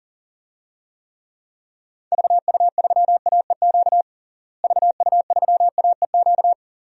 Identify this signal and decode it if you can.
Morse code, heard as a keyed tone, about 40 wpm, 700 Hz. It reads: VU3AEQ VU3AEQ
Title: DXCC-100 Call Signs - 6 Characters - Repeated 3x - 40wpm